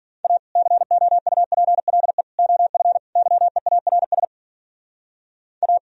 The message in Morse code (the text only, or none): A COUPLE OF YEARS A